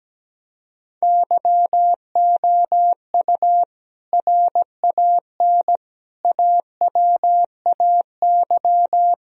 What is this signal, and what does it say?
Morse code, heard as a keyed tone, about 17 wpm, 700 Hz. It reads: YOU RAN AWAY
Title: Sentences from Top 1000 Words - Review Part 3 - 17wpm